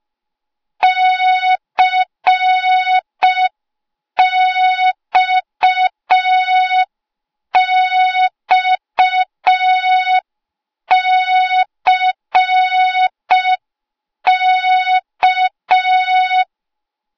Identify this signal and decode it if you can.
Morse code, heard as a keyed tone, about 5 wpm, 3700 Hz. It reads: CXXCK
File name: Morze-HA2MN / hang / 08KXC HA2MN